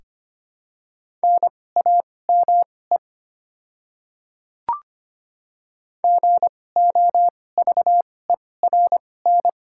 Morse code, heard as a keyed tone, about 25 wpm, spaced as for 18 wpm, 700 Hz. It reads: NAME GOVERN
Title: 2x Character Spacing - Top 500 Words - 25wpm